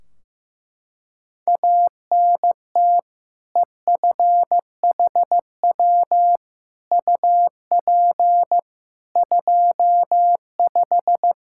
Morse code, heard as a keyed tone, about 15 wpm, 700 Hz. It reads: ANT EFHW UP 25
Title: Practice QSO - Exchange 2 - Info - 15wpm